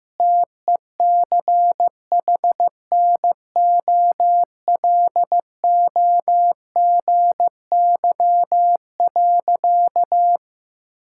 Morse code, heard as a keyed tone, about 15 wpm, 700 Hz. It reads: TECHNOLOGY.
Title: Sentences from Top 1000 Words - Review Part 5 - 15wpm